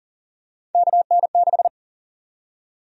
Morse code, heard as a keyed tone, about 40 wpm, 700 Hz. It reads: KN6
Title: USA Call Sign Prefixes - 40wpm